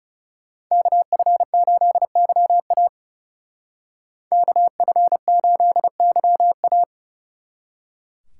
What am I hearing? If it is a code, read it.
Morse code, heard as a keyed tone, about 35 wpm, 700 Hz. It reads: KF8YA KF8YA
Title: Speed Racing - USA Call Signs 30wpm